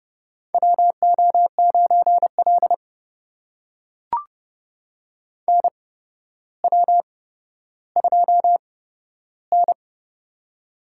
Morse code, heard as a keyed tone, about 31 wpm, spaced as for 13 wpm, 700 Hz. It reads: WO9L NW2N
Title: USA Call Signs - 4 Characters - 8x Character Spacing - 30wpm